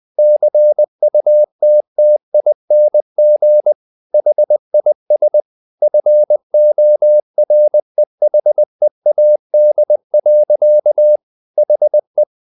Morse code, heard as a keyed tone, about 20 wpm, 600 Hz. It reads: CUTTING HIS FOREHEAD. HE
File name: the old man and the sea / the old man and the sea part 1 20wpm